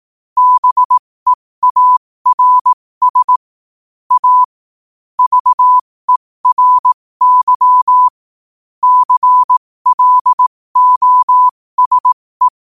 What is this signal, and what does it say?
Morse code, heard as a keyed tone, about 18 wpm, spaced as for 15 wpm, 1000 Hz. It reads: BEARS A VERY CLOSE
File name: qotdfiles / 15wpm